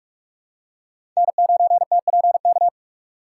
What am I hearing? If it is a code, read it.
Morse code, heard as a keyed tone, about 46 wpm, 700 Hz. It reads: N9TPK
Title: USA Call Signs - 5 Characters - 3x Character Spacing - 45wpm